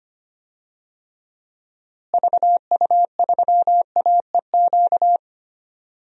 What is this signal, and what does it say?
Morse code, heard as a keyed tone, about 25 wpm, 700 Hz. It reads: VU3AEQ
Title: DXCC-100 Call Signs - 6 Characters - Repeated 3x - 25wpm